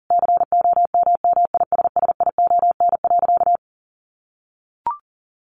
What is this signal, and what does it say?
Morse code, heard as a keyed tone, about 40 wpm, 700 Hz. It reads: COMMISSION.
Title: Sentences from Top 1400 Words 40wpm